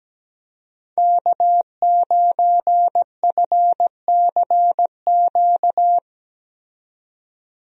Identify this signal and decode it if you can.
Morse code, heard as a keyed tone, about 17 wpm, 700 Hz. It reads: K9FCQ
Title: USA Call Signs - 5 Characters - 17wpm